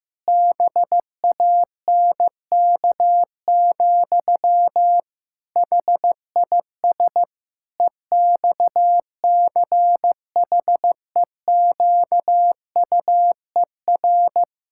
Morse code, heard as a keyed tone, about 16 wpm, 700 Hz. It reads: BANK, HIS EXCHEQUER
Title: Pangram Challenge v2 - 15wpm